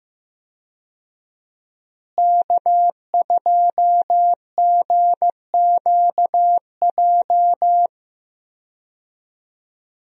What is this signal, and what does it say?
Morse code, heard as a keyed tone, about 15 wpm, 700 Hz. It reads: K2GQJ
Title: USA Call Signs - 5 Characters - Repeated 2x - 15wpm